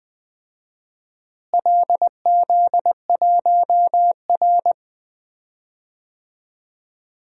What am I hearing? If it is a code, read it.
Morse code, heard as a keyed tone, about 20 wpm, 700 Hz. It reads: LZ1R
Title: DXCC-100 Call Signs - 4 Characters - Repeated 2x - 20wpm